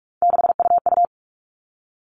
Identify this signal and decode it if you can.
Morse code, heard as a keyed tone, about 44 wpm, 700 Hz. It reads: 6UU
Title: All Cut Numbers - 3-Digit 45wpm